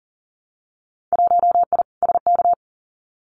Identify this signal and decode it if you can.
Morse code, heard as a keyed tone, about 40 wpm, 700 Hz. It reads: 1I SK
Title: Winter Field Day Exchanges Encore 40wpm